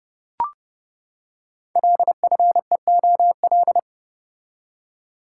Morse code, heard as a keyed tone, about 30 wpm, 700 Hz. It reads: LFEOL
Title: Code Groups - 5 Letters - 30wpm